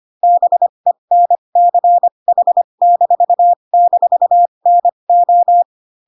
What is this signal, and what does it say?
Morse code, heard as a keyed tone, about 25 wpm, spaced as for 22 wpm, 700 Hz. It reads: BENCH--NO